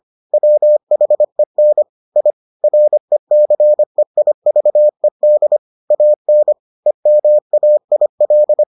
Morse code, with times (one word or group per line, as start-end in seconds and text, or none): 0.3-1.8 WHEN
2.2-2.3 I
2.6-5.6 RECEIVED
5.9-6.5 AN
6.9-8.6 EMAIL